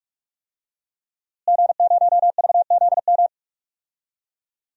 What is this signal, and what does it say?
Morse code, heard as a keyed tone, about 44 wpm, 700 Hz. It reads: G0VZM
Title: DXCC-100 Call Signs - 5 Characters - Repeated 3x - 45wpm